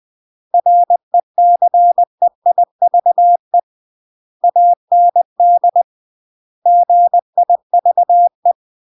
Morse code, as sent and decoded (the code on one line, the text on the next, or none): .-. -.-. .. ...- . / .- -. -.. / --. .. ...- .
RECEIVE AND GIVE